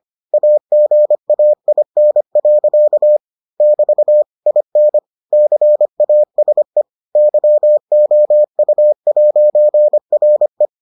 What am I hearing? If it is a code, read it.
Morse code, heard as a keyed tone, about 25 wpm, 600 Hz. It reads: AGAIN. <BT> IN CASE YOU'RE